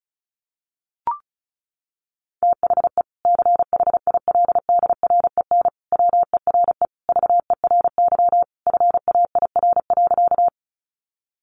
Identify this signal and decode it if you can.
Morse code, heard as a keyed tone, about 34 wpm, 700 Hz. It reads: THE CHILDREN WERE VERY FAIR.
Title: Sentences from Top 1300 Words 35wpm